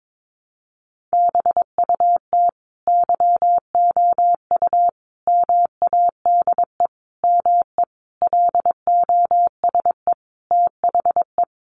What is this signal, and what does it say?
Morse code, heard as a keyed tone, about 22 wpm, 700 Hz. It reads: BUT YOU MADE ME LOSE THE
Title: Sentences from Top 1300 Words 22wpm